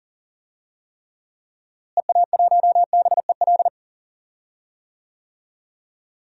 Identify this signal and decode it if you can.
Morse code, heard as a keyed tone, about 40 wpm, 700 Hz. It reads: EA1BEL
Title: DXCC-100 Call Signs - 6 Characters - Repeated 2x - 40wpm